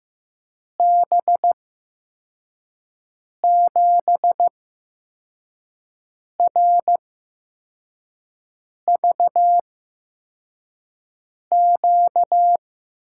Morse code, heard as a keyed tone, about 15 wpm, spaced as for 4 wpm, 700 Hz. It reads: B7RVQ
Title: USA Call Signs - 6 Characters - 8x Character Spacing - 15wpm